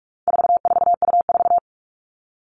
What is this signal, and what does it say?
Morse code, heard as a keyed tone, about 44 wpm, 700 Hz. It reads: 44U4
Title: All Cut Numbers - 4-Digit 45wpm